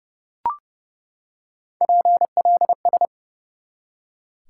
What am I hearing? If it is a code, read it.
Morse code, heard as a keyed tone, about 32 wpm, 700 Hz. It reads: PLS
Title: Top 90 QSO Elements 30wpm